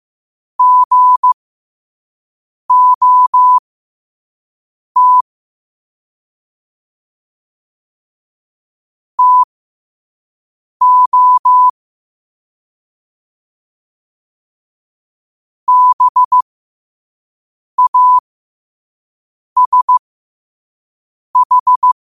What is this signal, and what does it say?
Morse code, heard as a keyed tone, about 15 wpm, spaced as for 5 wpm, 1000 Hz. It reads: GOT TO BASH